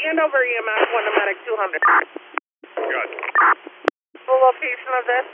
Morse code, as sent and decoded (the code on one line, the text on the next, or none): none